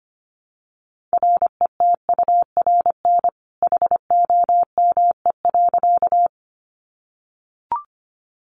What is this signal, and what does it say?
Morse code, heard as a keyed tone, about 25 wpm, 700 Hz. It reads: RETURN HOME.
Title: Sentences from Top 800 Words 25wpm